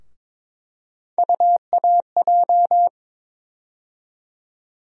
Morse code, heard as a keyed tone, about 22 wpm, 700 Hz. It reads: UAJ
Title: USA Call Sign Postfixes - 3 Letters 22wpm